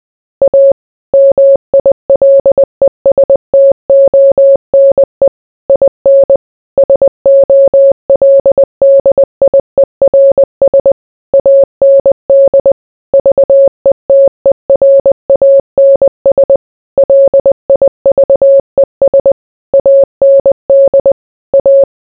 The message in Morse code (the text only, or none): A MILESTONE IN SOLDIERS AND VETERANS LIVES AND A